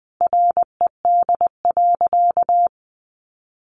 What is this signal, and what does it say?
Morse code, heard as a keyed tone, about 20 wpm, 700 Hz. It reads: RED.
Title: Easy Phrases 20wpm